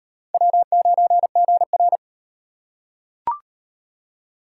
Morse code, heard as a keyed tone, about 37 wpm, 700 Hz. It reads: W9GR